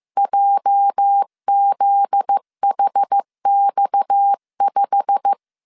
Morse code, heard as a keyed tone, about 15 wpm, 800 Hz. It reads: JZHX5
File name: CW / DET / 1 9 1500C S A 14wpm 3222